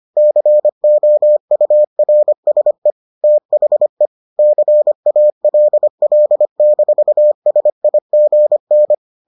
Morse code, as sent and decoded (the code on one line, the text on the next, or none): -.-. --- ..- .-. ... / - .... / -.-. .- .-.. .-.. -....- ... .. --. -.
COURSE THE CALL-SIGN